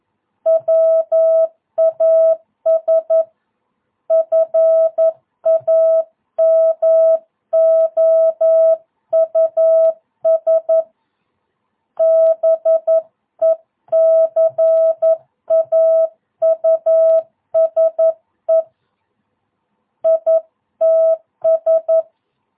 Morse code, heard as a keyed tone, about 11 wpm, 660 Hz. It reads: WAS FAMOUS BECAUSE ITS
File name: jh8cbh / 2022cw